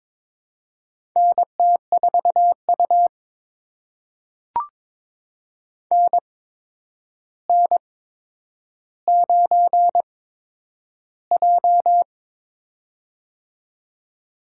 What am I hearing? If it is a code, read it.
Morse code, heard as a keyed tone, about 22 wpm, spaced as for 9 wpm, 700 Hz. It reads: NT4U NN9J
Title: USA Call Signs - 4 Characters - 8x Character Spacing - 22wpm